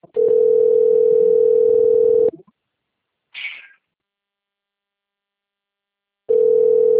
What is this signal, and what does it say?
Ringback tone (North America), 440 + 480 Hz.
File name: rg-1600-unknown- - - .15430